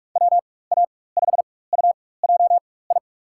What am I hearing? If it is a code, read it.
Morse code, heard as a keyed tone, about 46 wpm, spaced as for 21 wpm, 700 Hz. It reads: WA5UJI
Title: USA Call Signs - 6 Characters - 4x Character Spacing - 45wpm